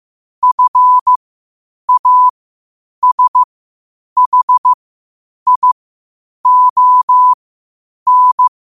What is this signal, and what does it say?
Morse code, heard as a keyed tone, about 15 wpm, spaced as for 8 wpm, 1000 Hz. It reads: FASHION
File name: qotdfiles / 8wpm